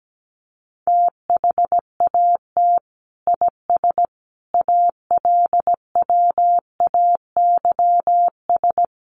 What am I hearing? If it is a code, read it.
Morse code, heard as a keyed tone, about 17 wpm, 700 Hz. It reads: THAT IS ALWAYS
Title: Sentences from Top 700 Words 17wpm